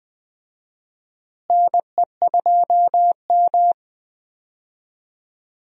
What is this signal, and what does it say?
Morse code, heard as a keyed tone, about 20 wpm, 700 Hz. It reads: NE2M
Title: USA Call Signs - 4 Characters - Repeated 2x - 20wpm